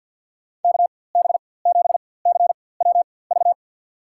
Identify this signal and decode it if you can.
Morse code, heard as a keyed tone, about 52 wpm, spaced as for 23 wpm, 700 Hz. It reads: KB7CWV
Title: USA Call Signs - 6 Characters - 4x Character Spacing - 50wpm